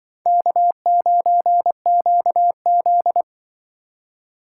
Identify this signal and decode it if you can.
Morse code, heard as a keyed tone, about 24 wpm, 700 Hz. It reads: K9QZ
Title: Speed Racing - USA 1x2 Call Signs 20wpm